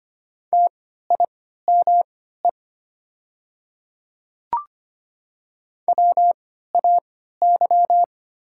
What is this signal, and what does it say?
Morse code, heard as a keyed tone, about 25 wpm, spaced as for 14 wpm, 700 Hz. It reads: TIME WAY
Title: 3x Character Spacing - Top 100 Words - 25wpm